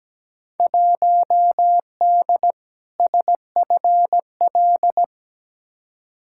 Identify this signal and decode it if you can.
Morse code, heard as a keyed tone, about 17 wpm, 700 Hz. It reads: 1D SFL
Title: Field Day Exchanges - 17wpm